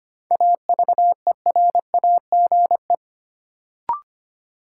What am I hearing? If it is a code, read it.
Morse code, heard as a keyed tone, about 25 wpm, 700 Hz. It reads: AVERAGE
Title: Sets of 5 Words - New Word Drill - 25wpm